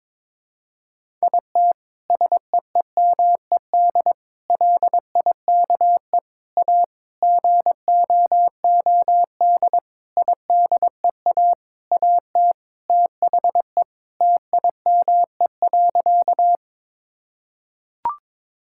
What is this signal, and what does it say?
Morse code, heard as a keyed tone, about 22 wpm, 700 Hz. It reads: IT SEEMED LIKE A GOOD IDEA AT THE TIME.
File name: Sentences with 9-Word Phrases - 22wpm